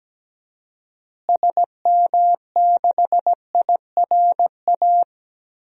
Morse code, heard as a keyed tone, about 17 wpm, 700 Hz. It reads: SM6IRA
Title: DXCC-100 Call Signs - 6 Characters - Repeated 3x - 17wpm